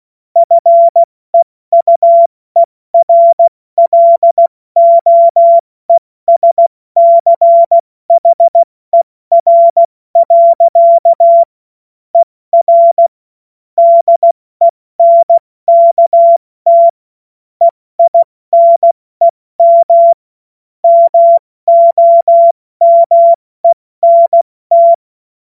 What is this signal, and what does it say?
Morse code, heard as a keyed tone, about 16 wpm, 680 Hz. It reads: FEUERLOESCHER. ER DENKT EINEM MOMENT
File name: Morsekurs / Witze / Witz 14. Juni 2022 70 F80